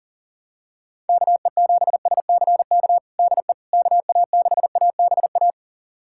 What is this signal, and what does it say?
Morse code, heard as a keyed tone, about 40 wpm, 700 Hz. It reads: KE7SCK DE KA6ABA
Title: USA Callsign DE Callsign - 6 Characters - 40wpm